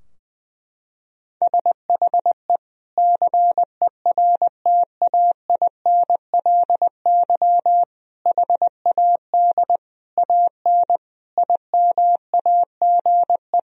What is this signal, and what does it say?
Morse code, heard as a keyed tone, about 20 wpm, 700 Hz. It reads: SHE CERTAINLY HAD AN IMAGE
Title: Sentences from Top 1000 Words - Review Part 4 - 20wpm